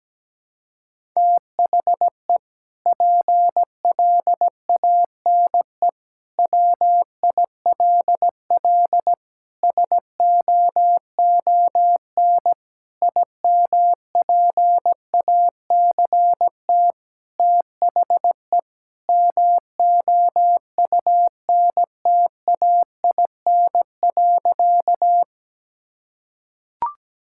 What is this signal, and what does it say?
Morse code, heard as a keyed tone, about 17 wpm, 700 Hz. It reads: THE PLANE WILL SOON IMPACT THE MOUNTAIN.
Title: Sentences from Top 1000 Words - Review Part 6 - 17wpm